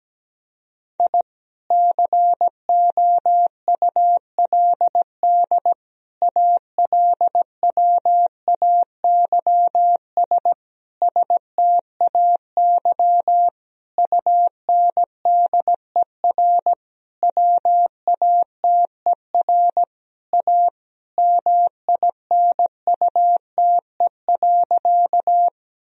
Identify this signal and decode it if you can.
Morse code, heard as a keyed tone, about 17 wpm, 700 Hz. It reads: I COULD ALWAYS STAY UNDER WATER A MINUTE.
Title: Sentences from Top 1000 Words - Review Part 1 - 17wpm